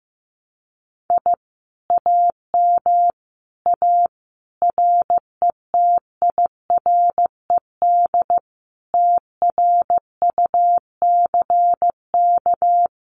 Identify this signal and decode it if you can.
Morse code, heard as a keyed tone, about 14 wpm, 700 Hz. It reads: I AM A RETIRED TRUCK
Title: Common QSO Phrases Encore 15wpm